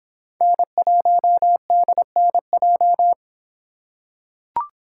Morse code, heard as a keyed tone, about 26 wpm, 700 Hz. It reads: N1DNJ